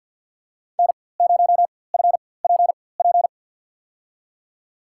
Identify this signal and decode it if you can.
Morse code, heard as a keyed tone, about 53 wpm, spaced as for 24 wpm, 700 Hz. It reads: N0FPP
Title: USA Call Signs - 5 Characters - 4x Character Spacing - 50wpm